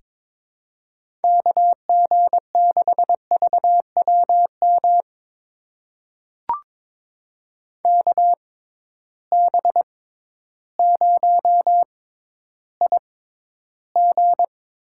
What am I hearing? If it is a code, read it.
Morse code, heard as a keyed tone, about 22 wpm, spaced as for 12 wpm, 700 Hz. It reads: KG6VWM KB0IG
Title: USA Call Signs - 6 Characters - 6x Character Spacing - 22wpm